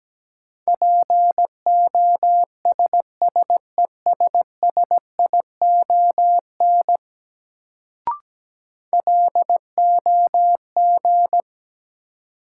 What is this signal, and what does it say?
Morse code, heard as a keyed tone, about 17 wpm, 700 Hz. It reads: POSSESSION LOG
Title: Sentences from Top 1400 Words - New Word Drill - 17wpm